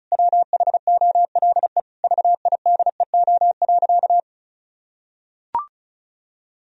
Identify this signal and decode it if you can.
Morse code, heard as a keyed tone, about 36 wpm, 700 Hz. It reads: WHOLE VIDEO.
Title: Sentences from Top 1000 Words - Review Part 4 - 35wpm